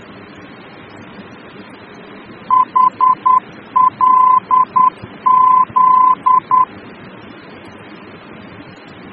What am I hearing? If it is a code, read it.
Morse code, heard as a keyed tone, about 10 wpm, 1020 Hz. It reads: HLZ
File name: germany / mx / HLZ-D-VOR